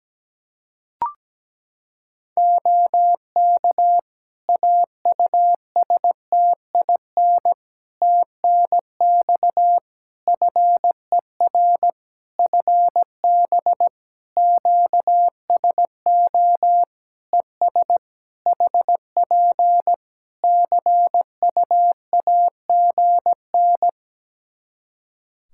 Morse code, heard as a keyed tone, about 17 wpm, 700 Hz. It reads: OK AUSTIN TNX FER FB QSO ES HP CUAGN
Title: Practice QSO - Exchange 4 - Ending - 17wpm